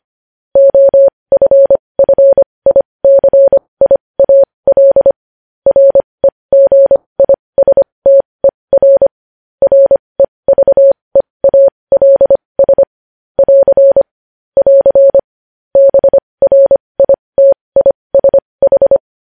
Morse code, heard as a keyed tone, about 25 wpm, spaced as for 20 wpm, 550 Hz. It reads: OFFICIAL REGISTER REVEALS <AR> <AR> BRITISH
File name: podcast / News headlines-1x-25x20